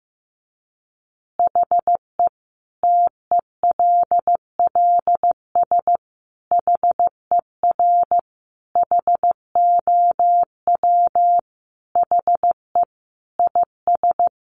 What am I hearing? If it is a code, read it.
Morse code, heard as a keyed tone, about 15 wpm, 700 Hz. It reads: HE TELLS HER HOW HE IS
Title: Additional Easy Phrases 15wpm